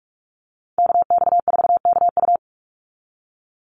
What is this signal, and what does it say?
Morse code, heard as a keyed tone, about 44 wpm, 700 Hz. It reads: KX4KU
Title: Call Signs - 45wpm